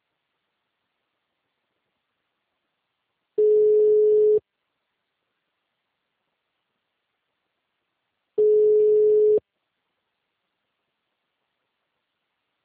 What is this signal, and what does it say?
Ringback tone (Europe, 425 Hz).
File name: out- -2029- - - .18959